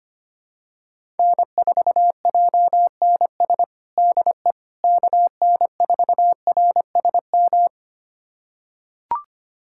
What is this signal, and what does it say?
Morse code, heard as a keyed tone, about 25 wpm, 700 Hz. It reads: N4JNS DE KN4RSM
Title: USA Callsign DE Callsign 25wpm